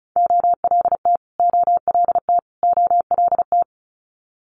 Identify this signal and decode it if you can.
Morse code, heard as a keyed tone, about 36 wpm, 700 Hz. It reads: OLT OLT OLT
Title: Multi-letter Phonemes Repeated 3x 35wpm